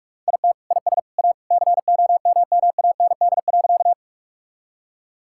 Morse code, heard as a keyed tone, about 41 wpm, 700 Hz. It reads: IT IS A COMMAND.
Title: Sentences from Top 1600 Words 45wpm